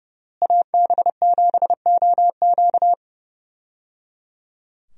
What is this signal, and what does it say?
Morse code, heard as a keyed tone, about 30 wpm, 700 Hz. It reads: AB7OQ
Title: USA Call Signs - 5 Characters - Repeated 3x - 30wpm